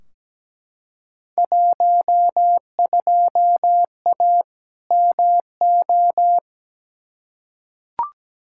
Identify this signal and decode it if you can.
Morse code, heard as a keyed tone, about 17 wpm, 700 Hz. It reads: 12A MO
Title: Field Day Exchanges Encore - 17wpm